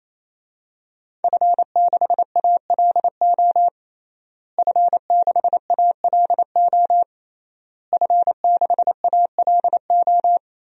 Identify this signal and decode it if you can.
Morse code, heard as a keyed tone, about 28 wpm, 700 Hz. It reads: F6ALO F6ALO F6ALO
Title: DXCC-100 Call Signs - 5 Characters - Repeated 3x - 28wpm